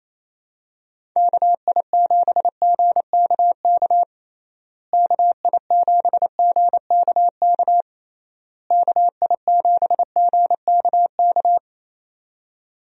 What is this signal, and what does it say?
Morse code, heard as a keyed tone, about 28 wpm, 700 Hz. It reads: KI7GKK KI7GKK KI7GKK
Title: USA Call Signs - 6 Characters - Repeated 3x - 28wpm